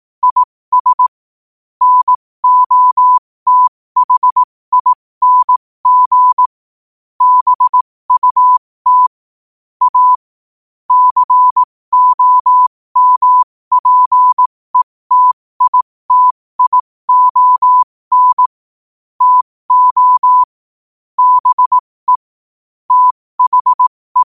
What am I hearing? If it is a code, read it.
Morse code, heard as a keyed tone, about 18 wpm, spaced as for 15 wpm, 1000 Hz. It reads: IS NOTHING BUT A COMPETITION TO BE THE
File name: qotdfiles / 15wpm